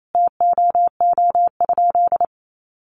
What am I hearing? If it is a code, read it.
Morse code, heard as a keyed tone, about 28 wpm, 700 Hz. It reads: TOO?
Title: Sentences from Top 200 Words 28wpm